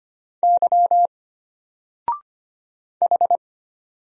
Morse code, heard as a keyed tone, about 25 wpm, 700 Hz. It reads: Y H
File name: Single-Letter-Number-25wpm